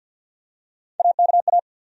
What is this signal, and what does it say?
Morse code, heard as a keyed tone, about 48 wpm, 700 Hz. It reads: AKA
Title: USA Call Sign Postfixes - 3 Letters 50wpm